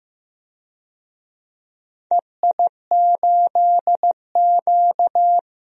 Morse code, heard as a keyed tone, about 15 wpm, 700 Hz. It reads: EI8Q
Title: DXCC-100 Call Signs - 4 Characters - Repeated 2x - 15wpm